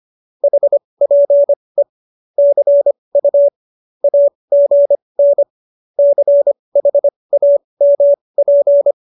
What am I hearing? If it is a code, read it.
Morse code, heard as a keyed tone, about 25 wpm, spaced as for 20 wpm, 570 Hz. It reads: HPE CU AGN CHAMP